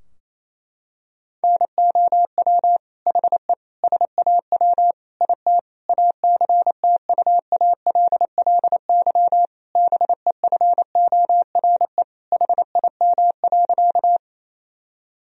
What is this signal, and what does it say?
Morse code, heard as a keyed tone, about 28 wpm, 700 Hz. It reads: NOW HE SAW IT ACTUALLY BEFORE HIM.
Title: Sentences from Top 1000 Words - Review Part 4 - 28wpm